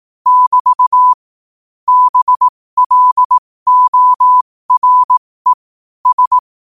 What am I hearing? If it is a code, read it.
Morse code, heard as a keyed tone, about 18 wpm, spaced as for 15 wpm, 1000 Hz. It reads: <BT> BLORE S
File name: qotdfiles / 15wpm